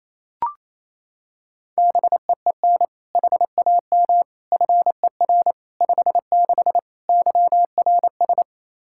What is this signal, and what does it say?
Morse code, heard as a keyed tone, about 28 wpm, 700 Hz. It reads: BEEN HAM FER 56 YRS
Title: Practice QSO - Exchange 3 - More Info - 28wpm